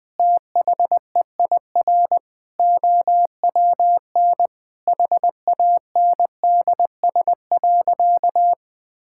Morse code, heard as a keyed tone, about 20 wpm, 700 Hz. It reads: THEIR OWN HANDS.
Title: Sentences with 8-Word Phrases - 20wpm